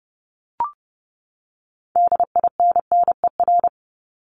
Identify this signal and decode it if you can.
Morse code, heard as a keyed tone, about 30 wpm, 700 Hz. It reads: DINNER
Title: Top 1751-2000 Words 30wpm